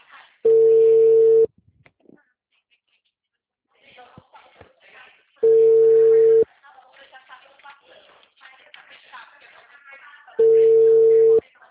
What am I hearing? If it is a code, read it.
Ringback tone (China, 450 Hz), with speech.
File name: out- -2025- - - .92